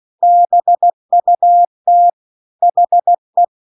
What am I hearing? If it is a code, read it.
Morse code, heard as a keyed tone, about 16 wpm, 700 Hz. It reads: BUT HE